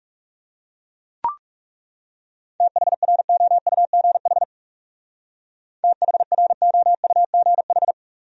Morse code, heard as a keyed tone, about 35 wpm, 700 Hz. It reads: THROUGH THROUGH
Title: Speed-Racing Top 200 Words 30wpm